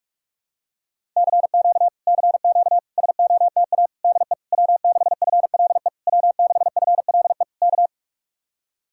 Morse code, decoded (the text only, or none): CQ CQ SOTA DE W6FLE W6FLE K